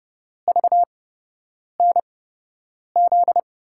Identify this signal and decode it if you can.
Morse code, heard as a keyed tone, about 30 wpm, spaced as for 8 wpm, 700 Hz. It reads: VNZ
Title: USA Call Signs - 6 Characters - 8x Character Spacing - 30wpm